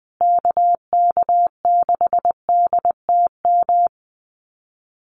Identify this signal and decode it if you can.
Morse code, heard as a keyed tone, about 20 wpm, 700 Hz. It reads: KK6DTM
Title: USA Call Signs Repeated 3x 20wpm